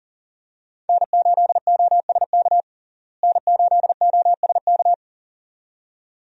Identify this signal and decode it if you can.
Morse code, heard as a keyed tone, about 40 wpm, 700 Hz. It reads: N8OSK N8OSK
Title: USA Call Signs - 5 Characters - Repeated 2x - 40wpm